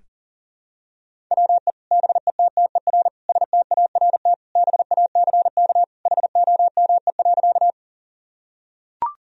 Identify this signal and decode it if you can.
Morse code, heard as a keyed tone, about 37 wpm, 700 Hz. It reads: WE BETTER START BACK HOME.
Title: Sentences from Top 1000 Words - Review Part 1 - 40wpm